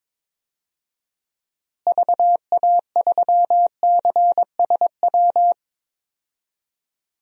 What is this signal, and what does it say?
Morse code, heard as a keyed tone, about 22 wpm, 700 Hz. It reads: VA3CSW
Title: DXCC-100 Call Signs - 6 Characters - Repeated 2x - 22wpm